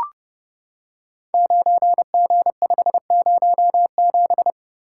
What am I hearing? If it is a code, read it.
Morse code, heard as a keyed tone, about 30 wpm, 700 Hz. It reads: 9G507
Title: Code Groups - 5 Letter:Numbers - 30wpm